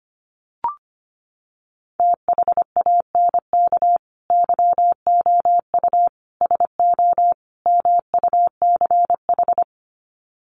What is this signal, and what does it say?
Morse code, heard as a keyed tone, about 25 wpm, 700 Hz. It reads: THANK YOU SO MUCH
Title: Sets of 4 Words 25wpm